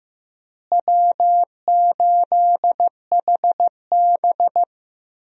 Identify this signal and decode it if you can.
Morse code, heard as a keyed tone, about 15 wpm, 700 Hz. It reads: W8HB
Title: USA Call Signs - 4 Characters - Repeated 3x - 15wpm